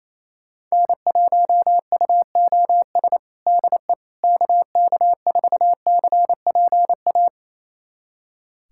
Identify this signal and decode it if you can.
Morse code, heard as a keyed tone, about 28 wpm, 700 Hz. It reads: N1UOS DE KK4CPA
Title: USA Callsign DE Callsign 28wpm